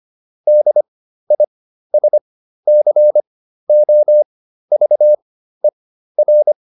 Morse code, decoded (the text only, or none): DISCOVER